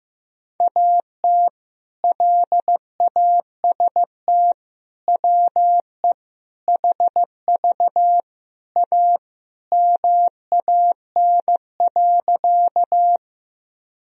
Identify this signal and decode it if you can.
Morse code, heard as a keyed tone, about 15 wpm, 700 Hz. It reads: AT LAST WE HV A MAN.
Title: Sentences from Top 300 Words and Abbreviations 15wpm